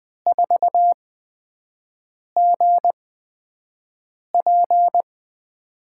Morse code, heard as a keyed tone, about 20 wpm, spaced as for 5 wpm, 700 Hz. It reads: 4GP
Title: USA Call Signs - 8x Character Spacing - 20wpm